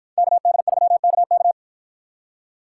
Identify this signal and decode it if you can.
Morse code, heard as a keyed tone, about 50 wpm, 700 Hz. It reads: KD3KK